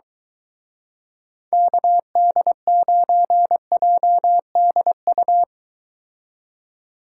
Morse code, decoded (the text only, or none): KD9JDU